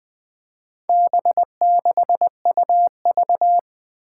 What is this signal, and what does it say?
Morse code, heard as a keyed tone, about 21 wpm, 700 Hz. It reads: B6UV
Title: Four Challenging Letter-Number Combinations 20wpm